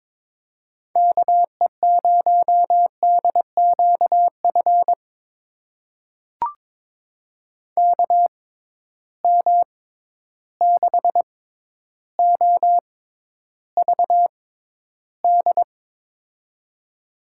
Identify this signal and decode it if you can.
Morse code, heard as a keyed tone, about 23 wpm, spaced as for 11 wpm, 700 Hz. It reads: KE0DQF KM6OVD
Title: USA Call Signs - 6 Characters - 6x Character Spacing - 22wpm